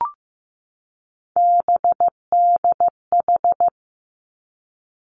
Morse code, heard as a keyed tone, about 15 wpm, 700 Hz. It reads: BDH
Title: Three Challenging Letter-Number Combinations 15wpm